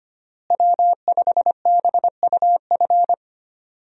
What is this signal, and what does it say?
Morse code, heard as a keyed tone, about 25 wpm, 700 Hz. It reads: W5BUF
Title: USA Call Signs - 5 Characters - Repeated 3x - 25wpm